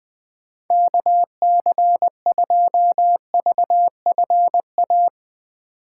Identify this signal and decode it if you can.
Morse code, heard as a keyed tone, about 21 wpm, 700 Hz. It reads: K C 2 V F A
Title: USA Call Signs - 6x Character Spacing - 20wpm